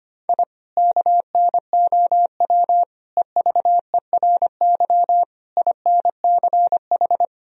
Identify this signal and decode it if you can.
Morse code, heard as a keyed tone, about 25 wpm, 700 Hz. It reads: I KNOW EVERY INCH
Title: Sentences from Top 1000 Words - Review Part 3 - 25wpm